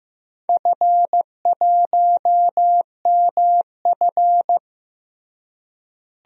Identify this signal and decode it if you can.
Morse code, heard as a keyed tone, about 15 wpm, 700 Hz. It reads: F1MF
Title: DXCC-100 Call Signs - 4 Characters - Repeated 3x - 15wpm